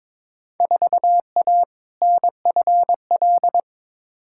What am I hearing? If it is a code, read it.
Morse code, heard as a keyed tone, about 21 wpm, 700 Hz. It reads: 4A NFL
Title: Field Day Exchanges - 22wpm